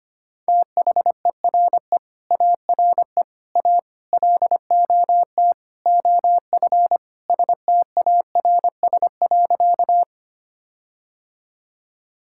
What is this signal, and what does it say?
Morse code, heard as a keyed tone, about 25 wpm, 700 Hz. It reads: THERE ARE A LOT OF STARS.
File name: practice / 25wpm / Sentences with 4-Word Phrases - 25wpm